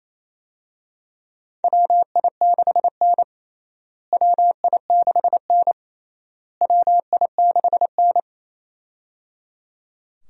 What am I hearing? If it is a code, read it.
Morse code, heard as a keyed tone, about 28 wpm, 700 Hz. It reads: WI6N WI6N WI6N
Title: USA Call Signs - 4 Characters - Repeated 3x - 28wpm